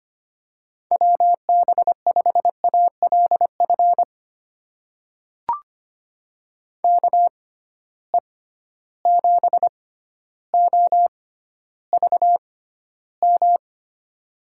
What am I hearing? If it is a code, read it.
Morse code, heard as a keyed tone, about 25 wpm, spaced as for 13 wpm, 700 Hz. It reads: WB5ALF KE7OVM
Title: USA Call Signs - 6 Characters - 6x Character Spacing - 25wpm